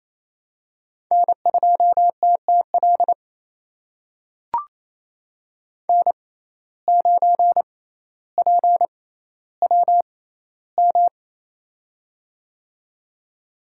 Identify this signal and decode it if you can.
Morse code, heard as a keyed tone, about 28 wpm, spaced as for 14 wpm, 700 Hz. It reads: N2TTL N9PWM
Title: USA Call Signs - 5 Characters - 6x Character Spacing - 28wpm